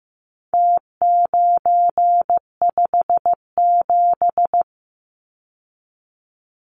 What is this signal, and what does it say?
Morse code, heard as a keyed tone, about 15 wpm, 700 Hz. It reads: T957
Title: Numbers - Contest Style - 4-Digit 15wpm